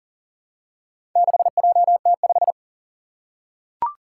Morse code, heard as a keyed tone, about 40 wpm, 700 Hz. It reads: 6JT5
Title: Four Challenging Letter-Number Combinations 40wpm